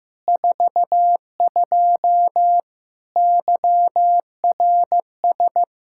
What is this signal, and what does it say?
Morse code, heard as a keyed tone, about 16 wpm, 700 Hz. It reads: 42 YRS
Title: Practice QSO - Exchange 3 - More Info - 15wpm